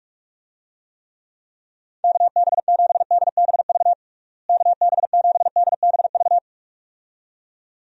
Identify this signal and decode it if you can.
Morse code, heard as a keyed tone, about 46 wpm, 700 Hz. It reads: KB7DBV KB7DBV
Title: USA Call Signs - 6 Characters - Repeated 2x - 45wpm